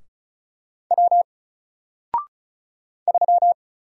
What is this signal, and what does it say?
Morse code, heard as a keyed tone, about 35 wpm, 700 Hz. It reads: W 3
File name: Single-Letter-Number-35wpm